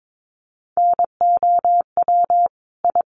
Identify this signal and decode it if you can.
Morse code, heard as a keyed tone, about 22 wpm, 700 Hz. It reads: NOW I
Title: Sentences from Top 1000 Words 22wpm